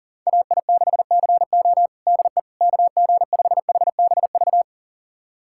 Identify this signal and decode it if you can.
Morse code, heard as a keyed tone, about 40 wpm, 700 Hz. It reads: AI6CO DE KG5HBV
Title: USA Callsign DE Callsign 40wpm